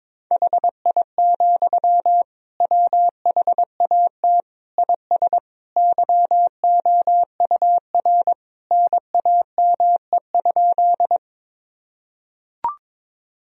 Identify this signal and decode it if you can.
Morse code, heard as a keyed tone, about 22 wpm, 700 Hz. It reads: HI, WHAT IS YOUR NAME?
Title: Sentences from Top 1000 Words - Review Part 3 - 22wpm